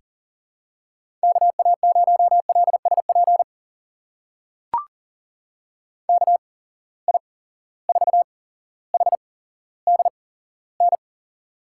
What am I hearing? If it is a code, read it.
Morse code, heard as a keyed tone, about 39 wpm, spaced as for 17 wpm, 700 Hz. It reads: KA0LSP KI4HDN
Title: USA Call Signs - 6 Characters - 8x Character Spacing - 40wpm